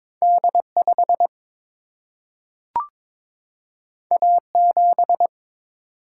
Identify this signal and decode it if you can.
Morse code, heard as a keyed tone, about 22 wpm, 700 Hz. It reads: D 5 A 7
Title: Single-Letter Single-Number 22wpm